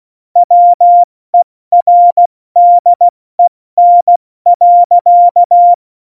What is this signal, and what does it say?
Morse code, heard as a keyed tone, about 16 wpm, 700 Hz. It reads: WERDEN.
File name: Morsekurs / Witze / Witz 08. September 2020, 70 F80